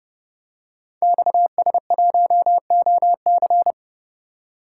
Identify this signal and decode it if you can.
Morse code, heard as a keyed tone, about 28 wpm, 700 Hz. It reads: XS1OC
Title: Code Groups - 5 Letter:Numbers - 30wpm